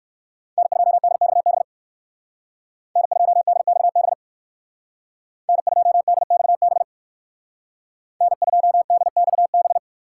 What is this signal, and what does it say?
Morse code, heard as a keyed tone, about 58 wpm, 700 Hz. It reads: N2DXB N2DXB N2DXB N2DXB